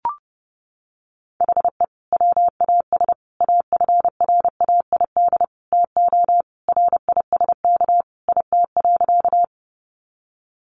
Morse code, heard as a keyed tone, about 30 wpm, 700 Hz. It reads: HE WAS AFRAID TO RISK IT.
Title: Sentences from Top 1500 Words 30wpm